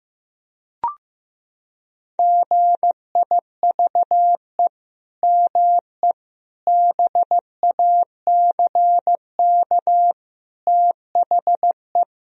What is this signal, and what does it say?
Morse code, heard as a keyed tone, about 16 wpm, 700 Hz. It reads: GIVE ME BACK THE
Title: Sentences from Top 1000 Words - Review Part 4 - 15wpm